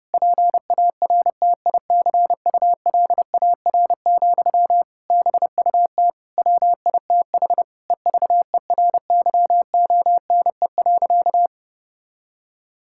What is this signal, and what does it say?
Morse code, heard as a keyed tone, about 32 wpm, 700 Hz. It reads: PARTICULAR, BUT WITH EVERYONE.
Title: Sentences from Top 1000 Words - Review Part 5 - 30wpm